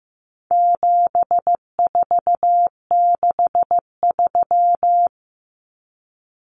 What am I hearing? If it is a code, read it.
Morse code, heard as a keyed tone, about 15 wpm, 700 Hz. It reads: 7463
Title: Common Cut Numbers - 4-Digit 15wpm